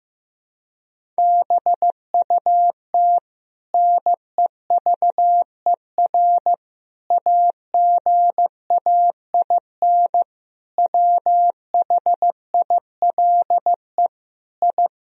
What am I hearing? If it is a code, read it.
Morse code, heard as a keyed tone, about 15 wpm, 700 Hz. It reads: BUT NEVER AGAIN WHILE I
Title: Sentences from Top 1000 Words - Review Part 1 - 15wpm